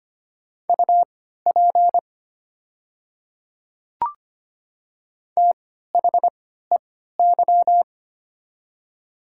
Morse code, heard as a keyed tone, about 25 wpm, spaced as for 14 wpm, 700 Hz. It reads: UP THEY
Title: 3x Character Spacing - Top 100 Words - 25wpm